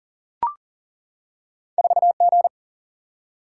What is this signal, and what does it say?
Morse code, heard as a keyed tone, about 39 wpm, 700 Hz. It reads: 4G
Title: Single-Letter Single-Number 40wpm